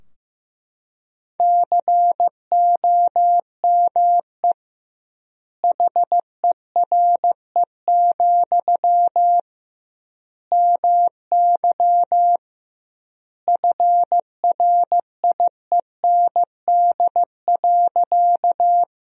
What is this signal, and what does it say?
Morse code, heard as a keyed tone, about 15 wpm, 700 Hz. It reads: COME HERE, MY FRIEND.
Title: 2x Word-Spacing - Sentences from Top 300 Words - 15wpm